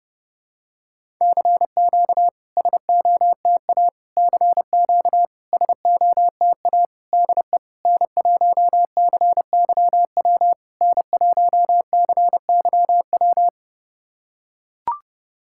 Morse code, heard as a keyed tone, about 28 wpm, 690 Hz. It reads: CQ SOTA CQ SOTA DE N1CYW N1CYW
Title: SOTA QSO Practice 30wpm